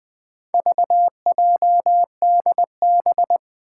Morse code, heard as a keyed tone, about 20 wpm, 700 Hz. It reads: VJDB
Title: Four Challenging Letter-Number Combinations 20wpm